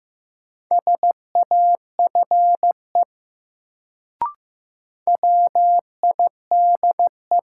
Morse code, heard as a keyed tone, about 15 wpm, 700 Hz. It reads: SAFE WIDE
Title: Sentences from Top 900 Words - New Word Drill - 15wpm